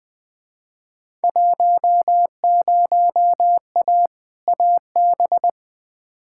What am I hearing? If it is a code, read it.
Morse code, heard as a keyed tone, about 21 wpm, 700 Hz. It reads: 10A AB
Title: Field Day Exchanges Encore - 20wpm